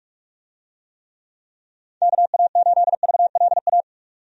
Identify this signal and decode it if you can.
Morse code, heard as a keyed tone, about 42 wpm, 700 Hz. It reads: KA8VLA
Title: USA Call Signs - 6 Characters - 45wpm